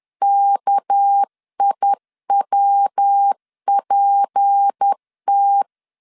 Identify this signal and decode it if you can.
Morse code, heard as a keyed tone, about 10 wpm, 800 Hz. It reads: KIWPT
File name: CW / DET / 1 6 1500C S B 10wpm 4137